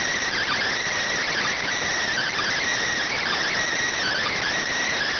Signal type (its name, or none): Emergency Alert System (SAME) tones